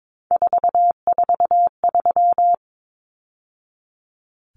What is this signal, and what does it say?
Morse code, heard as a keyed tone, about 22 wpm, 700 Hz. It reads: 443
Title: Numbers - 3-Digit 22wpm